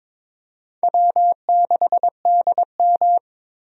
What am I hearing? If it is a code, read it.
Morse code, heard as a keyed tone, about 22 wpm, 700 Hz. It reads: W6DM
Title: USA Call Signs - 4 Characters - Repeated 3x - 22wpm